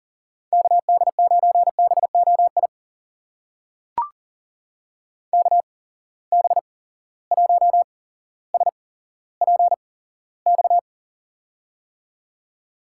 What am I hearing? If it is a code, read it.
Morse code, heard as a keyed tone, about 39 wpm, spaced as for 17 wpm, 700 Hz. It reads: KD9BOI KB1SPX
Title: USA Call Signs - 6 Characters - 8x Character Spacing - 40wpm